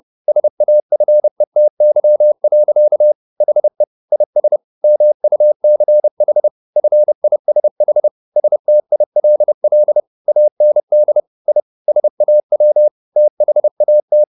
Morse code, heard as a keyed tone, about 30 wpm, 600 Hz. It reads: SAFETY. HE IS MUCH FISH STILL AND I SAW THAT